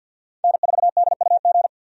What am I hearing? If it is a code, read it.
Morse code, heard as a keyed tone, about 50 wpm, 700 Hz. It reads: N4DUG
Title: USA Call Signs - 5 Characters - 50wpm